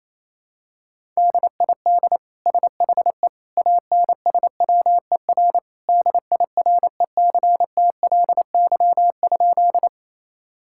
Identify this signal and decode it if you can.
Morse code, heard as a keyed tone, about 28 wpm, 700 Hz. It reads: DID SHE ANSWER DIRECTLY?
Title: Sentences from Top 1000 Words - Review Part 5 - 28wpm